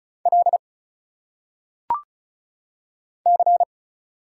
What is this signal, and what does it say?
Morse code, heard as a keyed tone, about 34 wpm, 700 Hz. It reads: L C
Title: Single Letters - Phonetic Pronunciation 35wpm